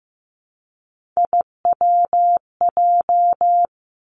Morse code, heard as a keyed tone, about 15 wpm, 700 Hz. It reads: IWJ
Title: Three Challenging Letter-Number Combinations 15wpm